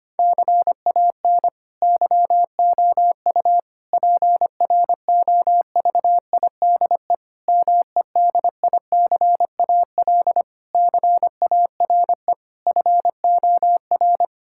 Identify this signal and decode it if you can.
Morse code, heard as a keyed tone, about 25 wpm, 700 Hz. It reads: CAN YOU PROVIDE MEDICAL CARE FOR
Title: Sentences from Top 1000 Words - Review Part 5 - 25wpm